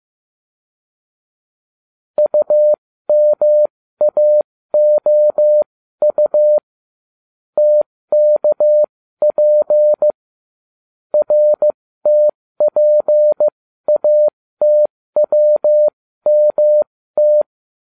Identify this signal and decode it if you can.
Morse code, heard as a keyed tone, about 15 wpm, spaced as for 13 wpm, 600 Hz. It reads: UMAOU TKP RTPATWMT